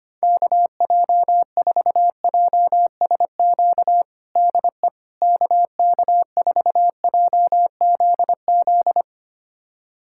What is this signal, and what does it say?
Morse code, heard as a keyed tone, about 25 wpm, 700 Hz. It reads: KJ4JSQ DE KK4JZZ
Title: USA Callsign DE Callsign 25wpm